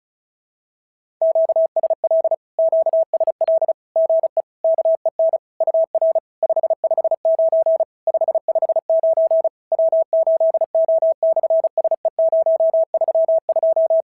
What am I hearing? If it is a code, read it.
Morse code, heard as a keyed tone, about 34 wpm, 650 Hz. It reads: QSL QSL GE KEN UR 559 559 W8O/SE032
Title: SOTA QSO Practice 35wpm